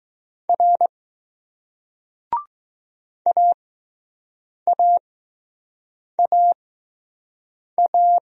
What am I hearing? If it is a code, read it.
Morse code, heard as a keyed tone, about 20 wpm, spaced as for 6 wpm, 700 Hz. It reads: R AAAA